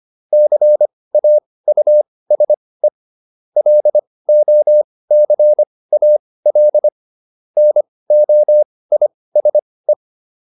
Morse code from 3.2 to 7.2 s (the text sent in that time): LOCAL